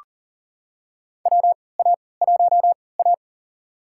Morse code, heard as a keyed tone, about 38 wpm, spaced as for 22 wpm, 700 Hz. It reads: WA1A
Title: USA Call Signs - 4 Characters - 3x Character Spacing - 40wpm